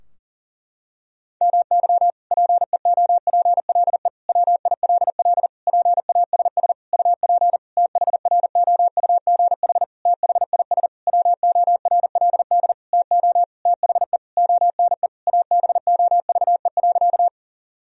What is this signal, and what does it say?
Morse code, heard as a keyed tone, about 37 wpm, 700 Hz. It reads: MY PEOPLE WILL PASS UP THROUGH THIS WORLD TO THE ONE ABOVE.
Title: Sentences from Top 1000 Words - Review Part 2 - 40wpm